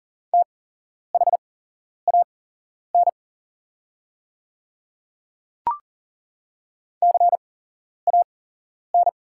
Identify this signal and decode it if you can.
Morse code, heard as a keyed tone, about 40 wpm, spaced as for 11 wpm, 700 Hz. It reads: THAN CAN
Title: 8x Character Spacing - Top 100 Words - 40wpm